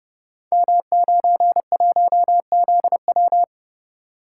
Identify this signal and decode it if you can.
Morse code, heard as a keyed tone, about 30 wpm, 700 Hz. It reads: M91ZW
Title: Code Groups - 5 Letter:Numbers - 30wpm